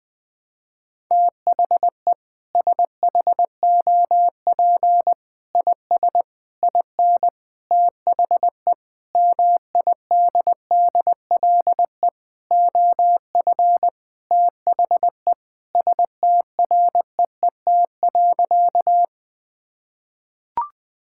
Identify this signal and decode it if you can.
Morse code, heard as a keyed tone, about 20 wpm, 700 Hz. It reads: THE SHOP IS IN THE MIDDLE OF THE STREET.
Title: Sentences with 4-Word Phrases - 20wpm